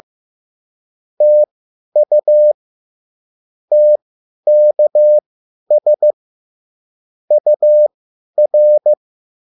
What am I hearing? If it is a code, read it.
Morse code, heard as a keyed tone, about 15 wpm, spaced as for 10 wpm, 600 Hz. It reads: TU TKS UR